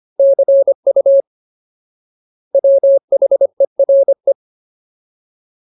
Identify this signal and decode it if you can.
Morse code, heard as a keyed tone, about 25 wpm, 550 Hz. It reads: CU WHERE
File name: Ultimate Word List / 25WPM Ultimate